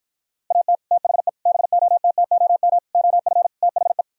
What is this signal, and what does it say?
Morse code, heard as a keyed tone, about 52 wpm, 700 Hz. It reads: AT THE BOTTOM OF THE